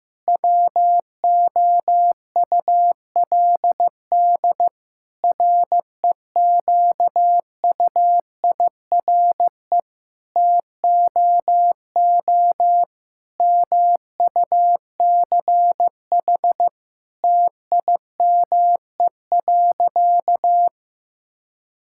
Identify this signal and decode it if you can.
Morse code, heard as a keyed tone, about 15 wpm, 700 Hz. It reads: WOULD REQUIRE TOO MUCH TIME.
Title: Sentences from Top 1000 Words - Review Part 6 - 15wpm